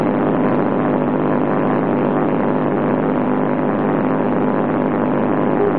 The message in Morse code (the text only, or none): none